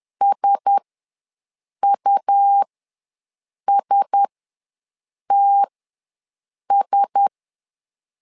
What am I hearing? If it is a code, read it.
Morse code, heard as a keyed tone, about 10 wpm, spaced as for 6 wpm, 800 Hz. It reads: SUSTS
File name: LEZIONI / 4 Q1S ws 10wpm 2208